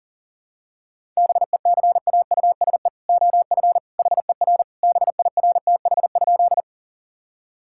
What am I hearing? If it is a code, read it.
Morse code, heard as a keyed tone, about 37 wpm, 700 Hz. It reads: BECAUSE OF HER BIRTH?
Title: Sentences from Top 1000 Words - Review Part 5 - 40wpm